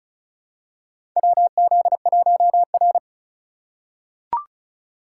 Morse code, heard as a keyed tone, about 34 wpm, 700 Hz. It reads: WZ1R
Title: USA Call Signs - 4 Characters - 8x Character Spacing - 35wpm